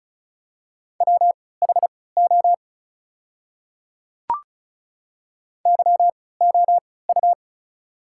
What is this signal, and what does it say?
Morse code, heard as a keyed tone, about 36 wpm, spaced as for 20 wpm, 700 Hz. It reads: WHO YOU
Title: 3x Character Spacing - Top 100 Words - 35wpm